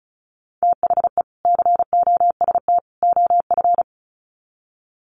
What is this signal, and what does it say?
Morse code, heard as a keyed tone, about 34 wpm, 700 Hz. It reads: THE COST OF
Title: Sets of 3 Words 35wpm